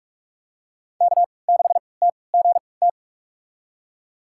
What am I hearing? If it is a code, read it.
Morse code, heard as a keyed tone, about 46 wpm, spaced as for 26 wpm, 700 Hz. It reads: K6TGT
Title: USA Call Signs - 3x Character Spacing - 45wpm